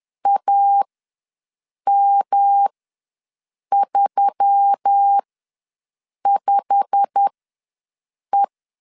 Morse code, heard as a keyed tone, about 10 wpm, spaced as for 6 wpm, 800 Hz. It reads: AM35E
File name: LEZIONI / 8 YJ45 / 8 YJ45 ws 10wpm 3843